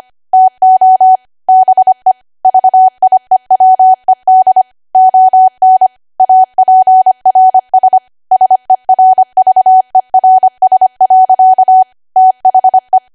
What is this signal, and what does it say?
Morse code, heard as a keyed tone, about 25 wpm, 750 Hz. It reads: TO BE VIEWED ON APRS SERVERS. THE
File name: Archive / 25 WPM / 25WPM